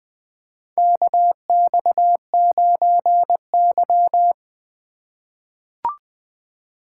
Morse code, heard as a keyed tone, about 20 wpm, 700 Hz. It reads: KX9Y
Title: USA Call Signs - 4 Characters - 3x Character Spacing - 20wpm